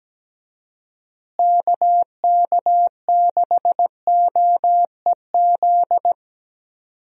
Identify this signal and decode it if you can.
Morse code, heard as a keyed tone, about 17 wpm, 700 Hz. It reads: KK6OEZ
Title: USA Call Signs - 6 Characters - Repeated 3x - 17wpm